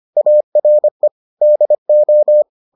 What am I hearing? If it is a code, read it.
Morse code, heard as a keyed tone, about 25 wpm, 600 Hz. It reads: ARE DO